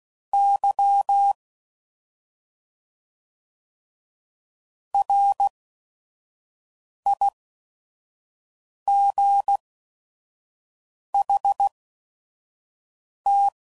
Morse code, heard as a keyed tone, about 16 wpm, spaced as for 5 wpm, 800 Hz. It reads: Y RIGHT
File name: Mixed / Misc / Words